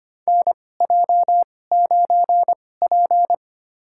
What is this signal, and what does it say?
Morse code, heard as a keyed tone, about 25 wpm, spaced as for 18 wpm, 700 Hz. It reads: NJ9P
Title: USA Call Signs - 4 Characters - 2x Character Spacing - 25wpm